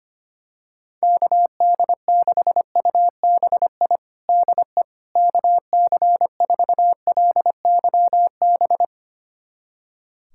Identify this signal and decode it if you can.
Morse code, heard as a keyed tone, about 25 wpm, 700 Hz. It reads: KD6UBI DE KC4LYB
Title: USA Callsign DE Callsign - 6 Characters - 25wpm